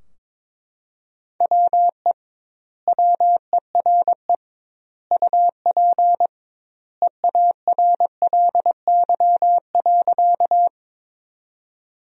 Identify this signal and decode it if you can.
Morse code, heard as a keyed tone, about 22 wpm, 700 Hz. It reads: WE WERE UP EARLY.
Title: 2x Word-Spacing - Sentences from Top 400 Words - 22wpm